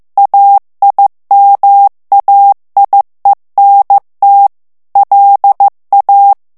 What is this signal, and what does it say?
Morse code, heard as a keyed tone, about 15 wpm, 800 Hz. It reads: AIMAIENT LA
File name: FRANCAIS TEXTE CLAIR 2 VITESSE 14 MOTS MINUTES